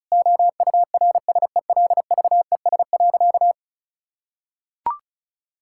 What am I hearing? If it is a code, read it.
Morse code, heard as a keyed tone, about 34 wpm, 700 Hz. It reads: OURSELVES.
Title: Sentences from Top 1600 Words 35wpm